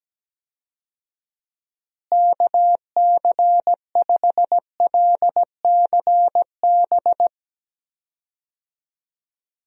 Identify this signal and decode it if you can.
Morse code, heard as a keyed tone, about 17 wpm, 700 Hz. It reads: KC5LCB
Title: USA Call Signs - 6 Characters - Repeated 2x - 17wpm